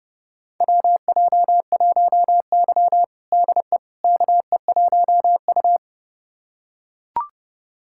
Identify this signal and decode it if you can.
Morse code, heard as a keyed tone, about 32 wpm, 700 Hz. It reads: WJ1Y DE KE1U
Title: USA Callsign DE Callsign - 4 Characters - 30wpm